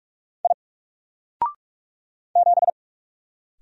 Morse code, heard as a keyed tone, about 44 wpm, 700 Hz. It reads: I 7
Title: Single Letter-Number Alt 45wpm